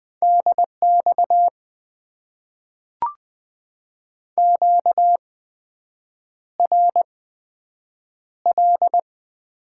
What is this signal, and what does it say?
Morse code, heard as a keyed tone, about 20 wpm, spaced as for 7 wpm, 700 Hz. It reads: DX QRL
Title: Top 90 QSO Elements - 8x Character Spacing - 20wpm